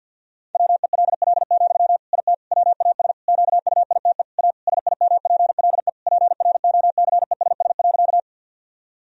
Morse code, heard as a keyed tone, about 52 wpm, 700 Hz. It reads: WELL, IT WAS QUITE A SIMPLE PROCESS.
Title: Sentences from Top 1000 Words - Review Part 3 - 50wpm